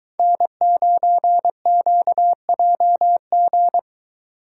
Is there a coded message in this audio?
Yes: Morse code from 0.2 to 3.8 s, heard as a keyed tone, about 22 wpm, 700 Hz. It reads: N9QJG